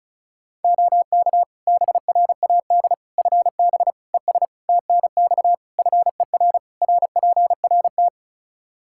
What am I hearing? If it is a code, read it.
Morse code, heard as a keyed tone, about 34 wpm, 700 Hz. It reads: OK BRAD FB ES TNX FER RPRT
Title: K1USN SST QSO Practice 35wpm